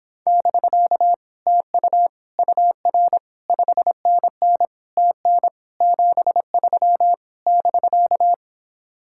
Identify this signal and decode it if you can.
Morse code, heard as a keyed tone, about 26 wpm, 700 Hz. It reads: <BK> TU UR 5NN TN 73 <BK>